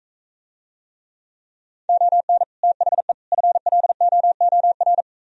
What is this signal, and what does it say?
Morse code, heard as a keyed tone, about 45 wpm, 700 Hz. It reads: ON THE FLOOR